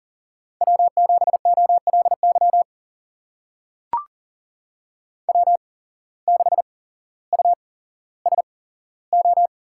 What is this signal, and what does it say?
Morse code, heard as a keyed tone, about 40 wpm, spaced as for 17 wpm, 700 Hz. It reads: W7OLY W6USO
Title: USA Call Signs - 5 Characters - 8x Character Spacing - 40wpm